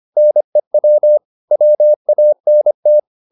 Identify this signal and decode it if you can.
Morse code, heard as a keyed tone, about 25 wpm, 600 Hz. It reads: NEW WANT
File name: Top-100-Words-25